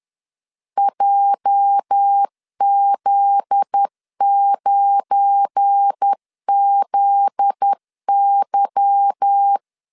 Morse code, heard as a keyed tone, about 10 wpm, 800 Hz. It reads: JZ9ZY
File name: LEZIONI / 10 SIGNS / 10A SLASH 10wpm 2808